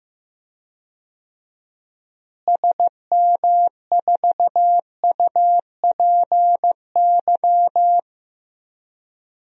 Morse code, heard as a keyed tone, about 15 wpm, 700 Hz. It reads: SM4UPY
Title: DXCC-100 Call Signs - 6 Characters - Repeated 2x - 15wpm